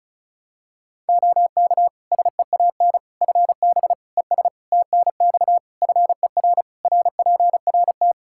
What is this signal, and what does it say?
Morse code, heard as a keyed tone, about 36 wpm, 700 Hz. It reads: OK SEAN FB ES TNX FER RPRT
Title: Practice QSO - Exchange 2 - Info - 35wpm